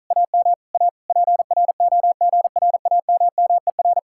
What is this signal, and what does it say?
Morse code, heard as a keyed tone, about 46 wpm, 700 Hz. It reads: AM A PROGRAMMER